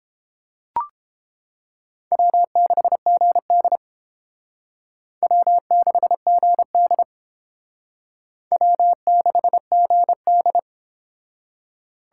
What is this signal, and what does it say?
Morse code, heard as a keyed tone, about 32 wpm, 700 Hz. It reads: W6GD W6GD W6GD